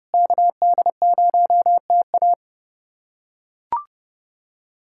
Morse code, heard as a keyed tone, about 30 wpm, 700 Hz. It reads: KD0TA
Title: USA Callsign DE Callsign 30wpm